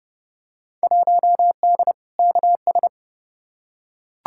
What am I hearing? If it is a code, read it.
Morse code, heard as a keyed tone, about 32 wpm, 700 Hz. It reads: 1D KS
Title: Field Day Exchanges - 30wpm